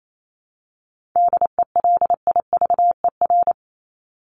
Morse code, heard as a keyed tone, about 28 wpm, 700 Hz. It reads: DELIVER
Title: Top 1751-2000 Words 28wpm